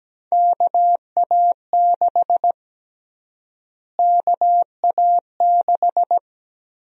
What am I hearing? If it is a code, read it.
Morse code, heard as a keyed tone, about 17 wpm, 700 Hz. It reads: KA6 KA6
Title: USA Call Sign Prefixes - Repeated 2x - 17wpm